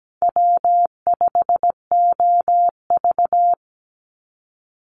Morse code, heard as a keyed tone, about 17 wpm, 700 Hz. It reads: W5OV
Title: Easy Call Signs Repeated 3x - 17wpm